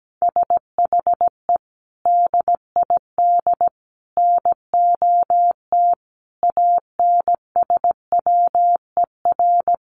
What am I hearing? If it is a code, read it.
Morse code, heard as a keyed tone, about 17 wpm, 700 Hz. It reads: SHE DID NOT ANSWER
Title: Sentences from Top 1100 Words 17wpm